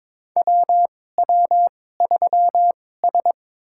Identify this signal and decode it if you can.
Morse code, heard as a keyed tone, about 22 wpm, spaced as for 16 wpm, 700 Hz. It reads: WW3S
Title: USA Call Signs - 4 Characters - 2x Character Spacing - 22wpm